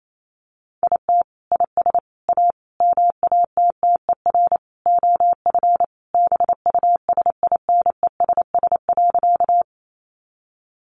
Morse code, heard as a keyed tone, about 28 wpm, 700 Hz. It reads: IT IS A MATTER OF BUSINESS.
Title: Sentences from Top 700 Words 28wpm